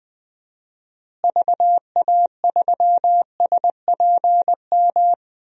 Morse code, heard as a keyed tone, about 21 wpm, 700 Hz. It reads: VA3SPM
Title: DXCC-100 Call Signs - 6 Characters - Repeated 3x - 20wpm